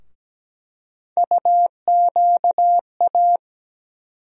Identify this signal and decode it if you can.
Morse code, heard as a keyed tone, about 17 wpm, 700 Hz. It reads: UQA
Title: USA Call Sign Postfixes - 3 Letters 17wpm